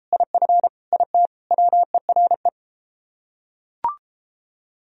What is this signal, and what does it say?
Morse code, heard as a keyed tone, about 30 wpm, 700 Hz. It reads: IF IT WERE